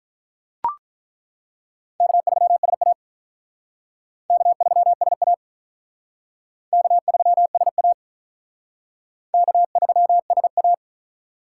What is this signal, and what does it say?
Morse code, heard as a keyed tone, about 42 wpm, 700 Hz. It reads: K3SA K3SA K3SA K3SA
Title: Speed Racing - USA 1x2 Call Signs 35wpm